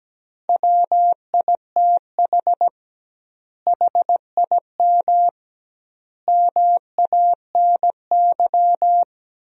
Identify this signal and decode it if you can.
Morse code, heard as a keyed tone, about 17 wpm, 700 Hz. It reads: WITH HIM MANY
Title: 2x Word-Spacing - Sentences from Top 200 Words - 17wpm